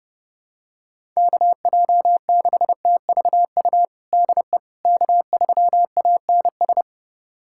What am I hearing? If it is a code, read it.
Morse code, heard as a keyed tone, about 30 wpm, 700 Hz. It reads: KJ6TVU DE K3ANS
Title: USA Callsign DE Callsign 30wpm